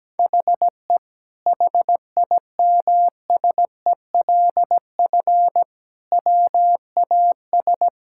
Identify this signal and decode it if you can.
Morse code, heard as a keyed tone, about 17 wpm, 700 Hz. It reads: HE HIMSELF WAS
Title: Sentences from Top 1600 Words 17wpm